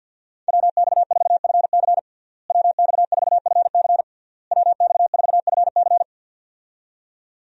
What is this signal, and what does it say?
Morse code, heard as a keyed tone, about 49 wpm, 700 Hz. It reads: WX4FC WX4FC WX4FC
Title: USA Call Signs - 5 Characters - Repeated 3x - 50wpm